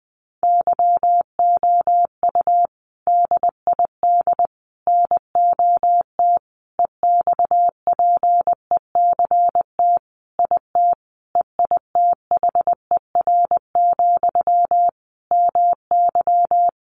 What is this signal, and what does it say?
Morse code, heard as a keyed tone, about 19 wpm, 700 Hz. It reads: YOU DID NOT EXPECT IT EITHER, MY
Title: Sentences from Top 1100 Words 20wpm